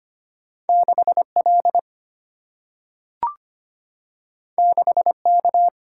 Morse code, heard as a keyed tone, about 25 wpm, 700 Hz. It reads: 6 L 6 K
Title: Single-Letter Single-Number 25wpm